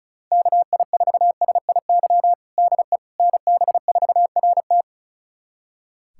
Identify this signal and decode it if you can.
Morse code, heard as a keyed tone, about 34 wpm, 700 Hz. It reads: KI4SIY DE NB4RT
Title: USA Callsign DE Callsign 35wpm